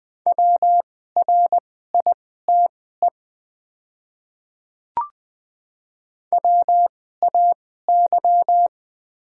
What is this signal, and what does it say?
Morse code, heard as a keyed tone, about 20 wpm, spaced as for 14 wpm, 700 Hz. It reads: WRITE WAY
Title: 2x Character Spacing - Top 100 Words - 20wpm